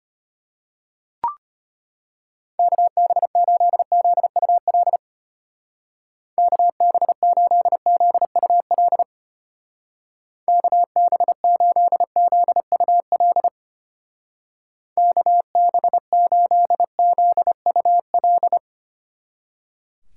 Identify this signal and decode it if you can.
Morse code, heard as a keyed tone, about 32 wpm, 700 Hz. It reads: KB8ZUL KB8ZUL KB8ZUL KB8ZUL